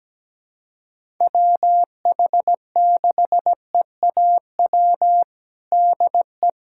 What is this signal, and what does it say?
Morse code, heard as a keyed tone, about 17 wpm, 700 Hz. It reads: WH6EAW DE
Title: USA Callsign DE Callsign 17wpm